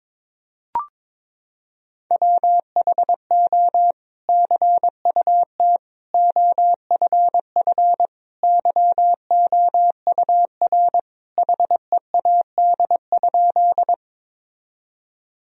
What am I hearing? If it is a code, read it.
Morse code, heard as a keyed tone, about 22 wpm, 700 Hz. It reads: WHO CUT OFF YOUR HEAD?
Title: Sentences from Top 1000 Words - Review Part 1 - 22wpm